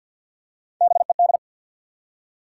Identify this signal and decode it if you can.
Morse code, heard as a keyed tone, about 51 wpm, 700 Hz. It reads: BED
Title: Three Letter Words - 50wpm